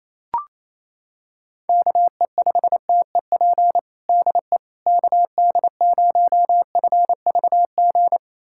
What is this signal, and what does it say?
Morse code, heard as a keyed tone, about 28 wpm, 700 Hz. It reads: KE5TEP DE KD0FVG
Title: USA Callsign DE Callsign - 6 Characters - 28wpm